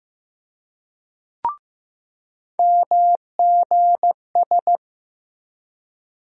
Morse code, heard as a keyed tone, about 15 wpm, 700 Hz. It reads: MGS
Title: USA Call Sign Postfixes - 3 Letters 15wpm